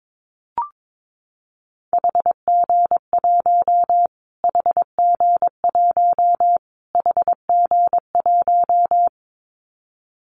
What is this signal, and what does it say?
Morse code, heard as a keyed tone, about 22 wpm, 700 Hz. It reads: HG1 HG1 HG1
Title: Callsign Prefixes for Top 25 DXCC - Repeated 3x - 22wpm